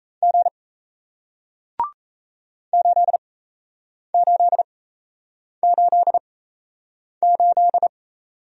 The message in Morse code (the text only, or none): G 8888